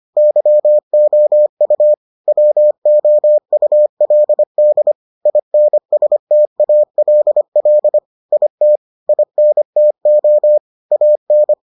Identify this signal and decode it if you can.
Morse code, heard as a keyed tone, about 25 wpm, 600 Hz. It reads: YOU WOULD INSTALL IT INTO AN